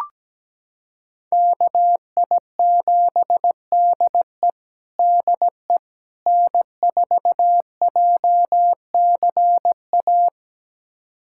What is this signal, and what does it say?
Morse code, heard as a keyed tone, about 17 wpm, 700 Hz. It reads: KI7DE DE N4JCA
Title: USA Callsign DE Callsign - 5 Characters - 17wpm